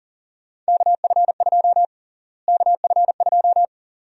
Morse code, heard as a keyed tone, about 40 wpm, 700 Hz. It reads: KF2 KF2
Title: USA Call Sign Prefixes - Repeated 2x - 40wpm